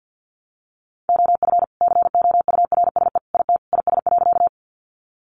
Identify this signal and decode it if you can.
Morse code, heard as a keyed tone, about 50 wpm, 700 Hz. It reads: OF COURSE IT IS.
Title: Sentences from Top 500 Words 50wpm